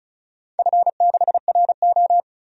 Morse code, heard as a keyed tone, about 36 wpm, 700 Hz. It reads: F6RO
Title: DXCC-100 Call Signs - 4 Characters - 35wpm